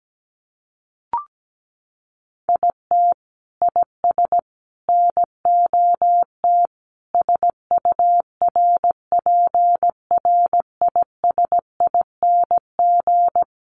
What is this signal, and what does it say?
Morse code, heard as a keyed tone, about 17 wpm, 700 Hz. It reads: IT IS NOT SURPRISING
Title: Sets of 5 Words 17wpm